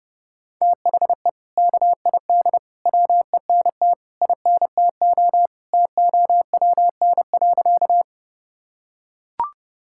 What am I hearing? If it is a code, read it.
Morse code, heard as a keyed tone, about 30 wpm, 700 Hz. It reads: THE KID WENT INTO TOWN.
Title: Sentences from Top 1600 Words 30wpm